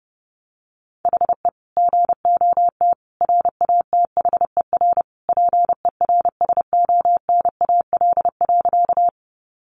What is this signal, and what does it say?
Morse code, heard as a keyed tone, about 30 wpm, 700 Hz. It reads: HE GOT RATHER PERSONAL.
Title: Sentences from Top 700 Words 30wpm